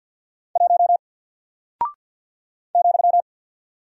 Morse code, heard as a keyed tone, about 48 wpm, 700 Hz. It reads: 1 ,
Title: Single Character (Letter:Number:Punctuation:Prosign) Alt 50wpm